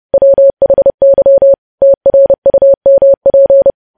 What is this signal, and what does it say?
Morse code, heard as a keyed tone, about 30 wpm, 550 Hz. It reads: WHY TRUMP
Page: news headlines in morse code at 30 wpm